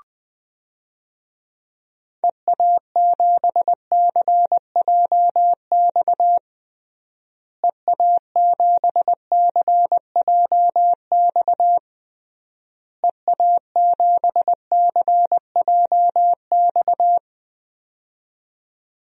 Morse code, heard as a keyed tone, about 20 wpm, 700 Hz. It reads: EA7CJX EA7CJX EA7CJX
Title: DXCC-100 Call Signs - 6 Characters - Repeated 3x - 20wpm